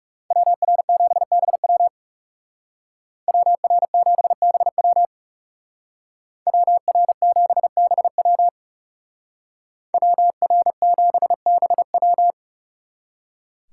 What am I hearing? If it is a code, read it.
Morse code, heard as a keyed tone, about 37 wpm, 700 Hz. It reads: WR7BW WR7BW WR7BW WR7BW
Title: Speed Racing - USA 2x2 Call Signs 30wpm